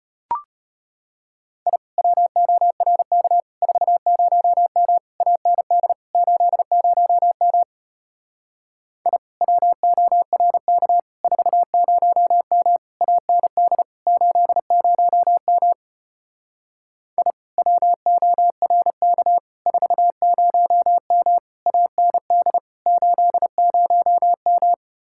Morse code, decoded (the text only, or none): I WORK 40M AND 80M I WORK 40M AND 80M I WORK 40M AND 80M